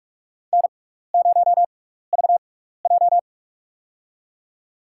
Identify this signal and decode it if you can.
Morse code, heard as a keyed tone, about 46 wpm, spaced as for 15 wpm, 700 Hz. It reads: N0VJ
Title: USA Call Signs - 4 Characters - 6x Character Spacing - 45wpm